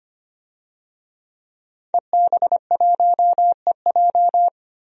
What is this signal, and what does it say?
Morse code, heard as a keyed tone, about 25 wpm, 700 Hz. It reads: EB1EJ
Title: DXCC-100 Call Signs - 5 Characters - 25wpm